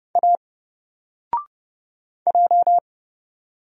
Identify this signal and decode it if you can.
Morse code, heard as a keyed tone, about 30 wpm, 700 Hz. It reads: A J
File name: Single-Letter-Number-30wpm